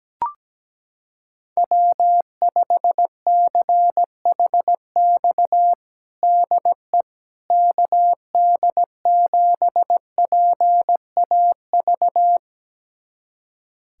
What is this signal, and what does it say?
Morse code, heard as a keyed tone, about 17 wpm, 700 Hz. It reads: W5CHX DE KD7PAV
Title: USA Callsign DE Callsign 17wpm